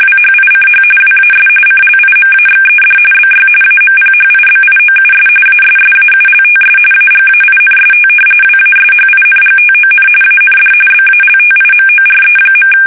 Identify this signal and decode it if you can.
RTTY, about 45.45 baud. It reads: WHICH WAS CLIMAXED BY AMPHIBIOUS ASSAULTS AGAINST WASHINGTON AND BALTIMORE.
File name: fsk850 / usnavy2